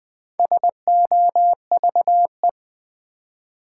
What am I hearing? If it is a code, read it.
Morse code, heard as a keyed tone, about 20 wpm, 700 Hz. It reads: SOVE
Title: Build A Word - 4 Letters At A Time - 11-Letter Words - 20wpm